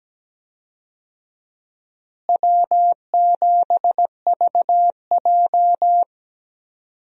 Morse code, heard as a keyed tone, about 17 wpm, 700 Hz. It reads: W7VJ
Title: USA Call Signs - 4 Characters - Repeated 2x - 17wpm